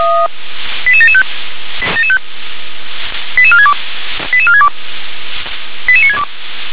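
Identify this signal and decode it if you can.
Five-tone selective call (ZVEI), heard as quick runs of five tones, tones of about 70 ms. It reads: 88775 88462 88462 88962
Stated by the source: ZVEI five-tone call received on an SDR.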